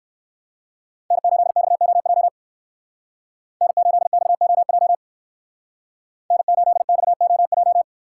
Morse code, heard as a keyed tone, about 58 wpm, 700 Hz. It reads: N8XOJ N8XOJ N8XOJ